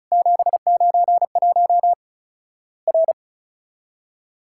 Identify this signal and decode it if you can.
Morse code, heard as a keyed tone, about 36 wpm, 700 Hz. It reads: 791 R
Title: Worked All Europe DX Contest - QTC Exchange - 35wpm